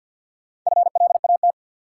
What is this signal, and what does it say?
Morse code, heard as a keyed tone, about 52 wpm, 700 Hz. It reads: FLAT
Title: Sentences from Top 1000 Words - New Word Drill - 50wpm